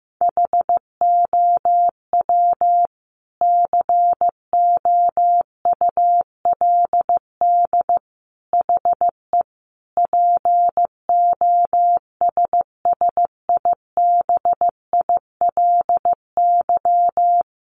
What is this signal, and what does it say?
Morse code, heard as a keyed tone, about 15 wpm, 700 Hz. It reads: HOW COULD HE POSSIBILY
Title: Sentences from Top 1300 Words 15wpm